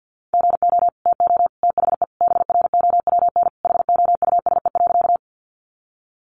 Morse code, heard as a keyed tone, about 50 wpm, 700 Hz. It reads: GO TO THE BROWN HOUSE.
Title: Sentences from Top 1200 Words 50wpm